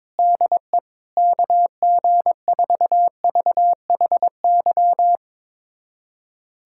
Morse code, heard as a keyed tone, about 22 wpm, 700 Hz. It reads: DE KG4VHY
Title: USA Callsign DE Callsign 22wpm